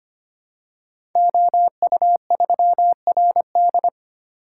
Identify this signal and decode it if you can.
Morse code, heard as a keyed tone, about 25 wpm, 700 Hz. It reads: OU3RD
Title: Code Groups - 5 Letter:Numbers - 25wpm